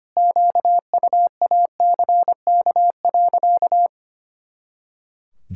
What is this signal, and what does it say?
Morse code, heard as a keyed tone, about 25 wpm, 700 Hz. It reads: QUACK.
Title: Pangram Challenge v2 - 25wpm